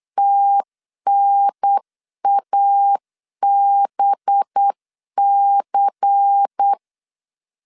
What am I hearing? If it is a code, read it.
Morse code, heard as a keyed tone, about 8 wpm, 800 Hz. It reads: TNABC
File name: CW / DET / 1 6 1500C S C 8wpm 5219